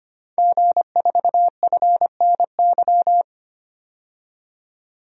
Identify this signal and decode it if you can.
Morse code, heard as a keyed tone, about 25 wpm, 700 Hz. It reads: G4FNY
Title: DXCC-100 Call Signs - 5 Characters - 25wpm